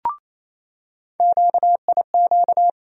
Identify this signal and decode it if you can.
Morse code, heard as a keyed tone, about 28 wpm, 700 Hz. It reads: QIQ
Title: USA Call Sign Postfixes - 3 Letters 28wpm